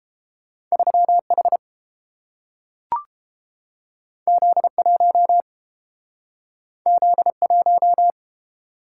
Morse code, heard as a keyed tone, about 30 wpm, 700 Hz. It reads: 3H Z1 Z1